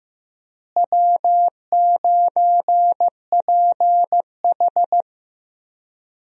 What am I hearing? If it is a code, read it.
Morse code, heard as a keyed tone, about 15 wpm, 700 Hz. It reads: W9PH
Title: USA Call Signs - 4 Characters - Repeated 3x - 15wpm